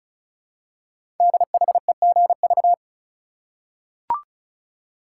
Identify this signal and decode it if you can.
Morse code, heard as a keyed tone, about 36 wpm, 700 Hz. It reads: DHEGV
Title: Code Groups - 5 Letters - 35wpm